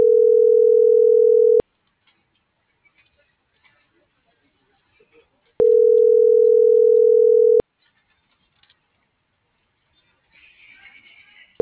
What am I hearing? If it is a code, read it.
Ringback tone (North America), 440 + 480 Hz.